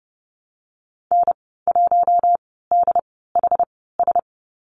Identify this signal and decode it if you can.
Morse code, heard as a keyed tone, about 30 wpm, spaced as for 17 wpm, 700 Hz. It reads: N1DHS
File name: Lesson 15g - Number 5 - Call Signs 30wpm